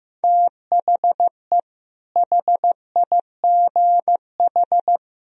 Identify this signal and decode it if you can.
Morse code, heard as a keyed tone, about 16 wpm, 700 Hz. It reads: THE HIGH
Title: Sentences from Top 1000 Words - Review Part 1 - 15wpm